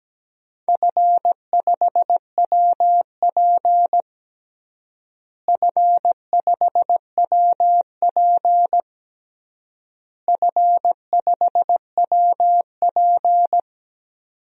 Morse code, heard as a keyed tone, about 17 wpm, 700 Hz. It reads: F5WP F5WP F5WP
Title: DXCC-100 Call Signs - 4 Characters - Repeated 3x - 17wpm